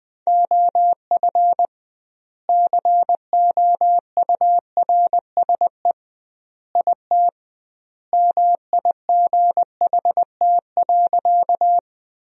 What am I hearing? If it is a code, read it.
Morse code, heard as a keyed tone, about 20 wpm, 700 Hz. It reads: OF COURSE IT MIGHT.
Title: 2x Word-Spacing - Sentences from Top 500 Words - 20wpm